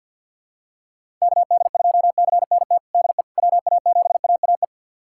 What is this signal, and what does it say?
Morse code, heard as a keyed tone, about 51 wpm, 700 Hz. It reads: KD2CNT DE WA7AAE
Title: USA Callsign DE Callsign - 6 Characters - 50wpm